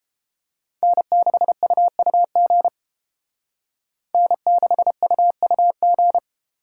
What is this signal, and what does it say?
Morse code, heard as a keyed tone, about 30 wpm, 700 Hz. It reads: N6UUG N6UUG